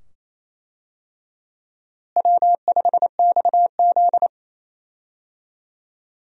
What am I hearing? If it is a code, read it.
Morse code, heard as a keyed tone, about 28 wpm, 700 Hz. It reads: W5XZ
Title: USA Call Signs - 4 Characters - Repeated 2x - 28wpm